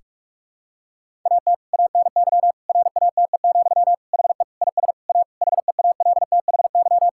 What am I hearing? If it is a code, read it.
Morse code, heard as a keyed tone, about 46 wpm, 700 Hz. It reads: AT ANY RATE, HE IS A HEALTHY
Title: Sentences from Top 1000 Words - Review Part 5 - 45wpm